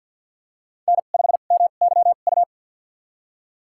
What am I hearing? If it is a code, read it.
Morse code, heard as a keyed tone, about 52 wpm, spaced as for 36 wpm, 700 Hz. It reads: N5MYU
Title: USA Call Signs - 5 Characters - 2x Character Spacing - 50wpm